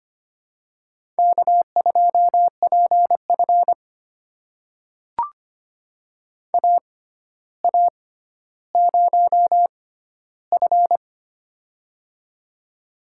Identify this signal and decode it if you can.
Morse code, heard as a keyed tone, about 25 wpm, spaced as for 13 wpm, 700 Hz. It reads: K2PF AA0F
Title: USA Call Signs - 4 Characters - 6x Character Spacing - 25wpm